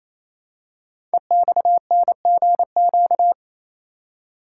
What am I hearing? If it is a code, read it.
Morse code, heard as a keyed tone, about 29 wpm, 700 Hz. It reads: EXNGQ
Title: Code Groups - 5 Letters - 28wpm